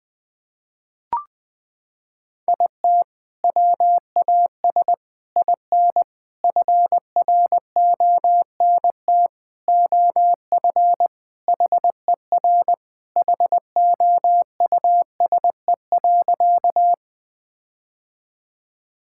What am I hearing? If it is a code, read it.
Morse code, heard as a keyed tone, about 21 wpm, 700 Hz. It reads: IT WAS IN FRONT OF HER HOUSE.
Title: Sentences with 4-Word Phrases - 20wpm